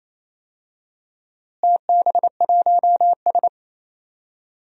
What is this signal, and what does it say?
Morse code, heard as a keyed tone, about 28 wpm, 700 Hz. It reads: TB1S
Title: DXCC-100 Call Signs - 4 Characters - Repeated 3x - 28wpm